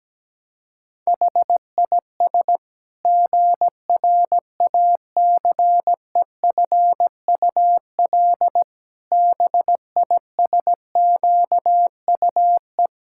Morse code, heard as a keyed tone, about 17 wpm, 700 Hz. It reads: HIS GRACEFUL BISQUE
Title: Pangram Challenge v2 - 17wpm